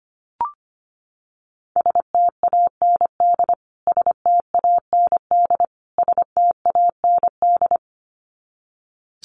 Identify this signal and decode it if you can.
Morse code, heard as a keyed tone, about 25 wpm, 700 Hz. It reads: STAND STAND STAND
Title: Top 500 Words Repeated 25wpm